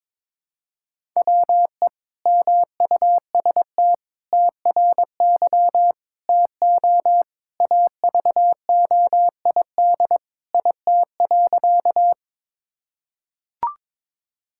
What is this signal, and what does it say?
Morse code, heard as a keyed tone, about 21 wpm, 700 Hz. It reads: WE MUST TRY TO AVOID IT.
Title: Sentences from Top 1000 Words - Review Part 6 - 22wpm